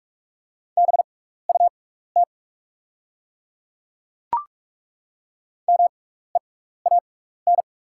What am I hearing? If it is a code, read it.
Morse code, heard as a keyed tone, about 43 wpm, spaced as for 15 wpm, 690 Hz. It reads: BUT MEAN
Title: 6x Character Spacing - Top 300 Words - 45wpm